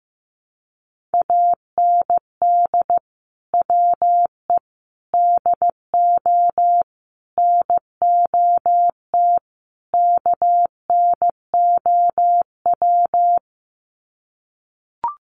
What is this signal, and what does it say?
Morse code, heard as a keyed tone, about 14 wpm, 700 Hz. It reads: AND WE DO NOT KNOW
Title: Sets of 5 Words Encore 15wpm